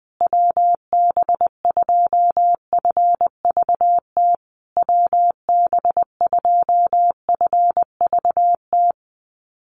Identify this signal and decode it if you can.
Morse code, heard as a keyed tone, about 20 wpm, 700 Hz. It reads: WB2FVT WB2FVT
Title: USA Call Signs Repeated 3x 20wpm